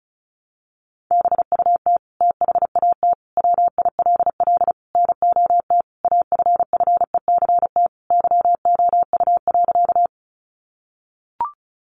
Morse code, heard as a keyed tone, about 34 wpm, 700 Hz. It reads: BUT THAT WILL NOT AFFECT YOU.
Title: Sentences from Top 900 Words 35wpm